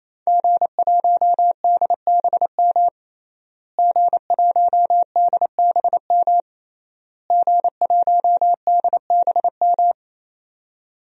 Morse code, heard as a keyed tone, about 28 wpm, 700 Hz. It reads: G1DBM G1DBM G1DBM
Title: DXCC-100 Call Signs - 5 Characters - Repeated 3x - 28wpm